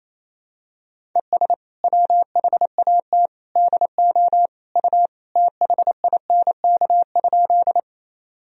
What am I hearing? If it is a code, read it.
Morse code, heard as a keyed tone, about 28 wpm, 700 Hz. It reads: ES WHAT DO U THINK?
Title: Sentences from Top 200 Words and Abbreviations 28wpm